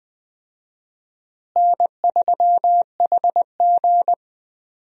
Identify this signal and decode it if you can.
Morse code, heard as a keyed tone, about 20 wpm, 700 Hz. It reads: N3HG
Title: USA Call Signs - 4 Characters - Repeated 3x - 20wpm